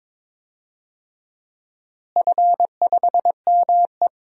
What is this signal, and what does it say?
Morse code, heard as a keyed tone, about 22 wpm, 700 Hz. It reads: F5ME
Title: DXCC-100 Call Signs - 4 Characters - Repeated 3x - 22wpm